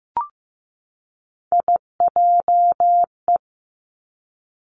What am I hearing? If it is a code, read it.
Morse code, heard as a keyed tone, about 15 wpm, 700 Hz. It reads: IJE
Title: Three Challenging Letter-Number Combinations 15wpm